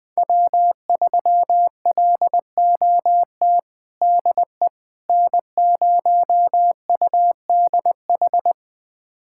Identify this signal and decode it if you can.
Morse code, heard as a keyed tone, about 21 wpm, 700 Hz. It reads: W3LOT DE N0UDH
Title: USA Callsign DE Callsign - 5 Characters - 20wpm